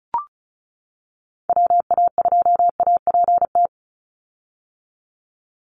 Morse code, heard as a keyed tone, about 34 wpm, 700 Hz. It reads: WA2APT
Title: USA Call Signs 35wpm